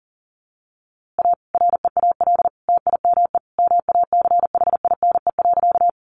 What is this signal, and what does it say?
Morse code, heard as a keyed tone, about 40 wpm, 700 Hz. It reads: A REAL TIME MACHINE.
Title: A Sound Of Thunder 40wpm